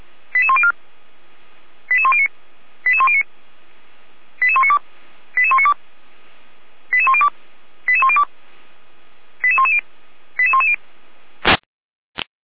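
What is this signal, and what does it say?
Five-tone selective call (ZVEI), heard as quick runs of five tones, tones of about 70 ms. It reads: 88104 88109 88109 88192 88192 88102 88102 88110 88110
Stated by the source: ZVEI five-tone call 88103 received on an SDR, recorded by feuerwehr-orlamuende (fire brigade Orlamünde).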